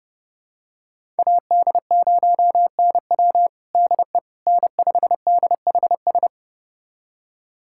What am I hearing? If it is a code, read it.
Morse code, heard as a keyed tone, about 30 wpm, 700 Hz. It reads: AD0NW DE N5DHS
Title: USA Callsign DE Callsign - 5 Characters - 30wpm